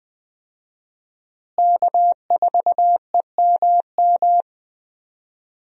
Morse code, heard as a keyed tone, about 20 wpm, 700 Hz. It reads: K4EMM
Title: USA Call Signs - 5 Characters - Repeated 2x - 20wpm